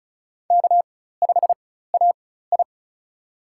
Morse code, heard as a keyed tone, about 36 wpm, spaced as for 16 wpm, 700 Hz. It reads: K5AI
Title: USA Call Signs - 4 Characters - 4x Character Spacing - 35wpm